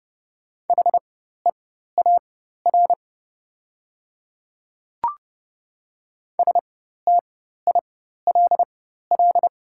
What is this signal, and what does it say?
Morse code, heard as a keyed tone, about 30 wpm, spaced as for 14 wpm, 700 Hz. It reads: HEAR STILL
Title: 4x Character Spacing - Top 300 Words - 30wpm